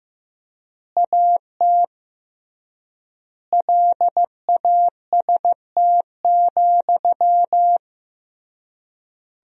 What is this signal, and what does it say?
Morse code, heard as a keyed tone, about 15 wpm, 700 Hz. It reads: AT LAST,
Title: 3x Word-Spacing - Sentences from Top 300 Words - 15wpm